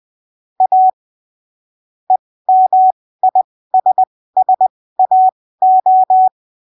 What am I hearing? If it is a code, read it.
Morse code, heard as a keyed tone, about 20 wpm, spaced as for 15 wpm, 750 Hz. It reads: A EMISSAO
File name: arquivos / texto / texto 18 20ppm